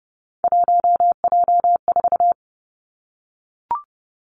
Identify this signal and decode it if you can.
Morse code, heard as a keyed tone, about 30 wpm, 700 Hz. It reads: 1J4
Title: Three Challenging Letter-Number Combinations 30wpm